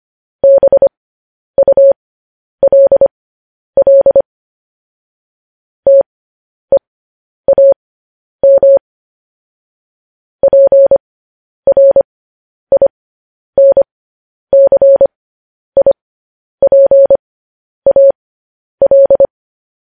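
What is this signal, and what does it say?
Morse code, heard as a keyed tone, about 25 wpm, spaced as for 10 wpm, 550 Hz. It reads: BULL TEAM PRINCIPAL
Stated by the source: Morse news headlines at 10.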